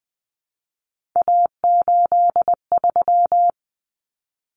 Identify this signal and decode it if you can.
Morse code, heard as a keyed tone, about 20 wpm, 700 Hz. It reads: A83
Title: Common Cut Numbers - 3-Digit 20wpm